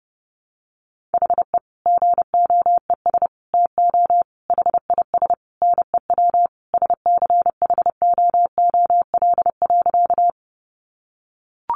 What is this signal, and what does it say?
Morse code, heard as a keyed tone, about 30 wpm, 700 Hz. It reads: HE GOES TO HIS NEW SCHOOL.
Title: Easy Phrases 30wpm